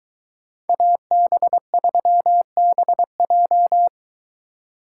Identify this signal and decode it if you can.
Morse code, heard as a keyed tone, about 23 wpm, 700 Hz. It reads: AB3BJ